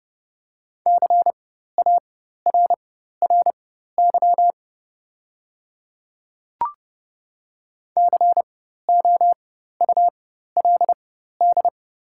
Morse code, heard as a keyed tone, about 30 wpm, spaced as for 14 wpm, 700 Hz. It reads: CARRY COULD
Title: 4x Character Spacing - Top 300 Words - 30wpm